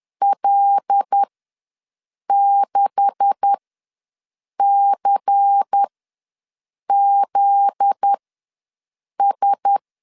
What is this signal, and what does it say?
Morse code, heard as a keyed tone, about 10 wpm, spaced as for 6 wpm, 800 Hz. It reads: L6CZS
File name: LEZIONI / 9 286 ws 10wpm 4032